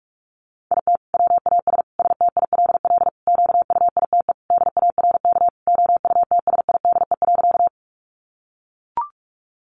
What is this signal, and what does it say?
Morse code, heard as a keyed tone, about 44 wpm, 700 Hz. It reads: IT WAS STILL QUITE DARK OUTSIDE.
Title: Sentences from Top 900 Words 45wpm